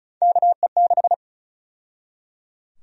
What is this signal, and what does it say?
Morse code, heard as a keyed tone, about 34 wpm, 700 Hz. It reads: KE6
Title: USA Call Sign Prefixes - 35wpm